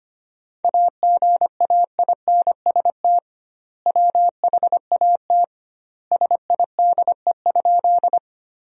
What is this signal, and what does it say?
Morse code, heard as a keyed tone, about 25 wpm, 700 Hz. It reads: AGAINST WHAT SIDE?
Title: 2x Word-Spacing - Sentences from Top 400 Words - 25wpm